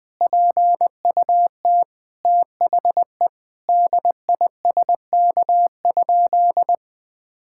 Morse code, heard as a keyed tone, about 20 wpm, 700 Hz. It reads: PUT THE DISK?
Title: Sentences from Top 1000 Words - Review Part 6 - 20wpm